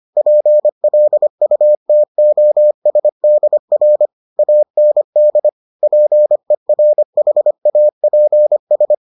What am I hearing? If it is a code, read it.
Morse code, heard as a keyed tone, about 25 wpm, 600 Hz. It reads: PLUTOSDR AND PERHAPS